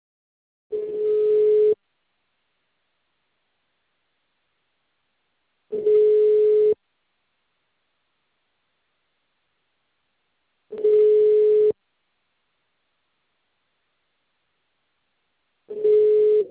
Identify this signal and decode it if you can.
Ringback tone (Europe, 425 Hz).